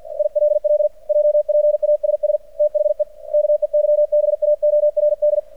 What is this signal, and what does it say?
Morse code, heard as a keyed tone, about 50 wpm, 600 Hz. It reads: ILL JPASS THE KEYBAORD
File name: archived / CRSnet